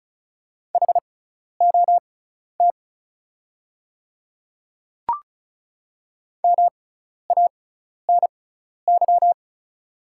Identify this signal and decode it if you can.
Morse code, heard as a keyed tone, about 36 wpm, spaced as for 12 wpm, 690 Hz. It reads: HOT MANY
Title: 6x Character Spacing - Top 300 Words - 35wpm